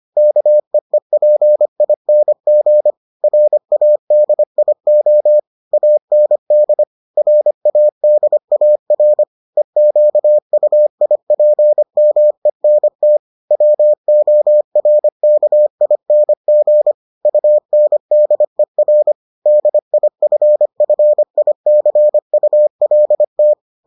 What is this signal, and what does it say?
Morse code, heard as a keyed tone, about 25 wpm, 600 Hz. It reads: KEEPING RADIO AND RADAR EQUIPMENT WORKING UNDER DIFFICULT